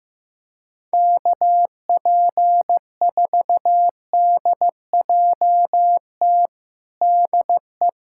Morse code, heard as a keyed tone, about 16 wpm, 700 Hz. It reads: KP4DJT DE
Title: USA Callsign DE Callsign - 6 Characters - 15wpm